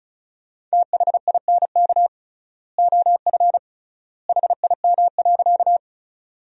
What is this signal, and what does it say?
Morse code, heard as a keyed tone, about 36 wpm, 700 Hz. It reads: THINK OF HIM.
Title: 3x Word-Spacing - Sentences from Top 200 Words - 35wpm